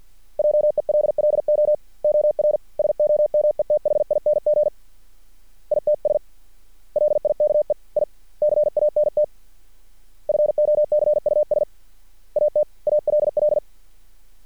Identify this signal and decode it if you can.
Morse code, heard as a keyed tone, about 50 wpm, 600 Hz. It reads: JELLO OR SOMETHING ITS LIKE I CANT FOCUS AT ALL
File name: archived / CRSnet